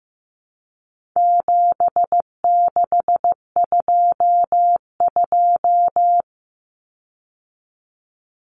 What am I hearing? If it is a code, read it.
Morse code, heard as a keyed tone, about 15 wpm, 700 Hz. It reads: 7622
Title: Common Cut Numbers - 4-Digit 15wpm